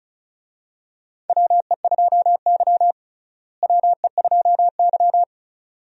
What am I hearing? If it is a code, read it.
Morse code, heard as a keyed tone, about 34 wpm, 700 Hz. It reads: WE2Y WE2Y
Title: USA Call Signs - 4 Characters - Repeated 2x - 35wpm